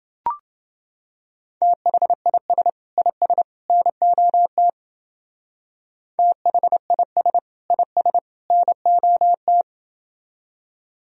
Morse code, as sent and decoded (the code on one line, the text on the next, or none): - .... .. ... / .. ... / -. --- - / - .... .. ... / .. ... / -. --- -
THIS IS NOT THIS IS NOT